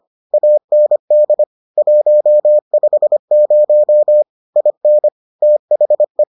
Morse code, heard as a keyed tone, about 25 wpm, 600 Hz. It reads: AND 150 IN THE